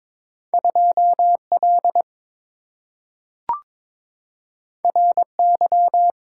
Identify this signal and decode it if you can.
Morse code, heard as a keyed tone, about 22 wpm, 700 Hz. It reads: 2L RY
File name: practice / 22wpm / Two Letter-Number Combinations - 22wpm